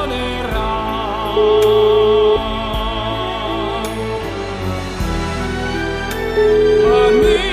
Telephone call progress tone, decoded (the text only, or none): ringback tone